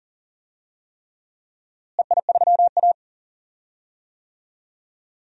Morse code, heard as a keyed tone, about 39 wpm, 700 Hz. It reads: EI3A
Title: DXCC-100 Call Signs - 4 Characters - Repeated 3x - 40wpm